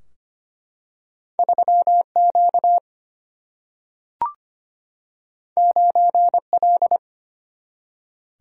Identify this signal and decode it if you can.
Morse code, heard as a keyed tone, about 25 wpm, 700 Hz. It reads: 3Q 9L
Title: Single-Letter Single-Number 25wpm